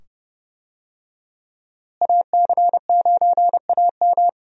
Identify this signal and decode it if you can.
Morse code, heard as a keyed tone, about 30 wpm, 700 Hz. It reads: AC9AM
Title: USA Call Signs - 5 Characters - Repeated 2x - 30wpm